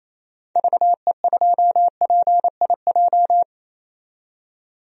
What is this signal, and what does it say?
Morse code, heard as a keyed tone, about 28 wpm, 700 Hz. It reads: VE2PIJ
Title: DXCC-100 Call Signs - 6 Characters - 28wpm